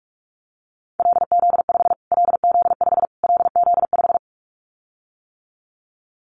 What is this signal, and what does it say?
Morse code, heard as a keyed tone, about 44 wpm, 700 Hz. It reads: LZ5 LZ5 LZ5
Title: Callsign Prefixes for Top 25 DXCC - Repeated 3x - 45wpm